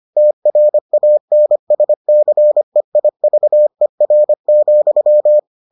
Morse code, heard as a keyed tone, about 25 wpm, 600 Hz. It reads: TRANSCEIVER,